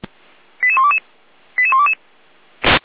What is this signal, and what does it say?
Five-tone selective call (ZVEI), heard as quick runs of five tones, tones of about 70 ms. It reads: 88122 88122